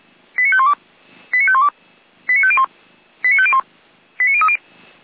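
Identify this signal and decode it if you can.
Five-tone selective call (ZVEI), heard as quick runs of five tones, tones of about 70 ms. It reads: 89512 89512 89501 89501 89030
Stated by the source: ZVEI five-tone selective call received on an SDR, from the feuerwehr-orlamuende.org fire brigade audio server.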